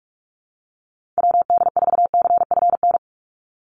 Morse code, heard as a keyed tone, about 46 wpm, 690 Hz. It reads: WD4CFN
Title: NA SOTA Chaser Callsigns 45wpm